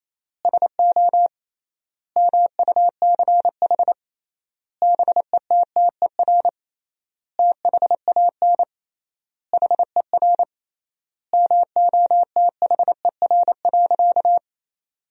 Morse code, heard as a keyed tone, about 28 wpm, 700 Hz. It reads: SO MUCH BETTER THAN HER MOTHER.
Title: 3x Word-Spacing - Sentences from Top 400 Words - 28wpm